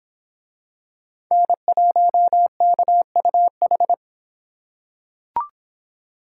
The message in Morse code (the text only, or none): N1KUH